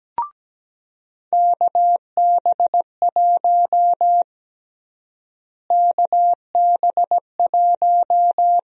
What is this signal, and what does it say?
Morse code, heard as a keyed tone, about 17 wpm, 700 Hz. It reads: KB1 KB1
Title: USA Call Sign Prefixes - Repeated 2x - 17wpm